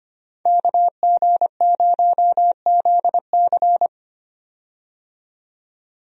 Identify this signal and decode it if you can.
Morse code, heard as a keyed tone, about 25 wpm, 700 Hz. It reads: KG0ZC
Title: USA Call Signs - 5 Characters - 25wpm